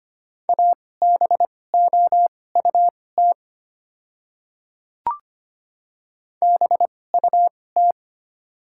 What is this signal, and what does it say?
Morse code, heard as a keyed tone, about 25 wpm, spaced as for 18 wpm, 690 Hz. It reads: ABOUT BUT
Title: 2x Character Spacing - Top 100 Words - 25wpm